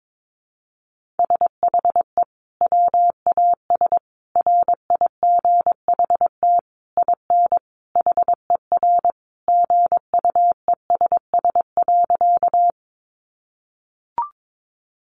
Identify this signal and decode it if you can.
Morse code, heard as a keyed tone, about 22 wpm, 700 Hz. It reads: SHE WAS RIGHT IN HER GUESS.
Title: Sentences from Top 1500 Words 22wpm